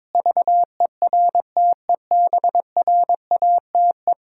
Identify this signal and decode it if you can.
Morse code, heard as a keyed tone, about 22 wpm, 700 Hz. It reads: VERTEBRATE
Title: Pangram Challenge v2 - 22wpm